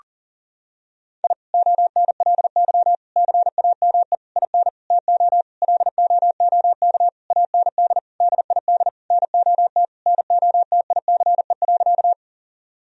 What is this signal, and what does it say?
Morse code, heard as a keyed tone, about 40 wpm, 700 Hz. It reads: I ONLY CAME IN TO LOOK AND DID NOT NOTICE.
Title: Sentences from Top 1000 Words - Review Part 2 - 40wpm